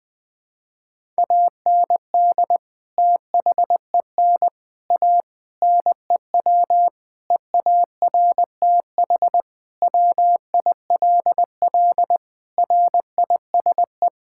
Morse code, heard as a keyed tone, about 19 wpm, 700 Hz. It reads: AND THEN A NEW EARTH WILL RISE
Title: Sentences from Top 1000 Words - Review Part 6 - 20wpm